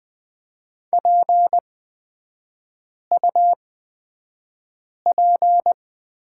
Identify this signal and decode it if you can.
Morse code, heard as a keyed tone, about 20 wpm, spaced as for 5 wpm, 690 Hz. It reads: PUP
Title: 229 Lesson 37e - Letter x - Three Characters 20wpm